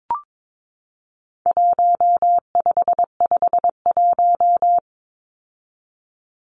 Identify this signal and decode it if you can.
Morse code, heard as a keyed tone, about 22 wpm, 700 Hz. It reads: 1551
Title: Numbers - Contest Style - 4-Digit 22wpm